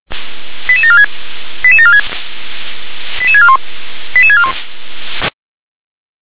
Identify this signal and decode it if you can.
Five-tone selective call (ZVEI), heard as quick runs of five tones, tones of about 80 ms. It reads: 88746 88746 88641 88641